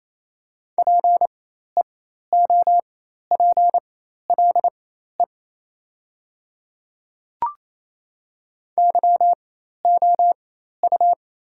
Morse code, heard as a keyed tone, about 28 wpm, spaced as for 13 wpm, 700 Hz. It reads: PEOPLE YOU
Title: 4x Character Spacing - Top 100 Words - 28wpm